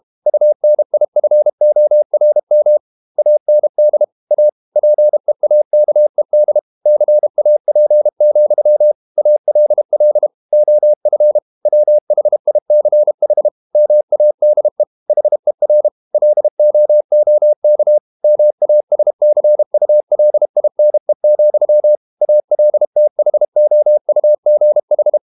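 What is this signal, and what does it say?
Morse code, heard as a keyed tone, about 32 wpm, 600 Hz. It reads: UNIFORM AND A PEAKED CAP, ALL OF WHICH MADE HER LOOK MASCULINE, ALTHOUGH
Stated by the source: Morse code practice at 32 wpm from CW med Gnister, reading Ken Follett's novel Never in English.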